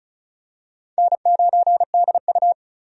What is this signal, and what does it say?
Morse code, heard as a keyed tone, about 34 wpm, 700 Hz. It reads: N9DU
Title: USA Call Signs - 4 Characters - 35wpm